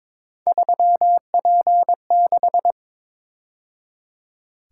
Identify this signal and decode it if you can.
Morse code, heard as a keyed tone, about 26 wpm, 700 Hz. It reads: 3P6
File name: Speed-Racing - Three Letter-Number Combinations - 22wpm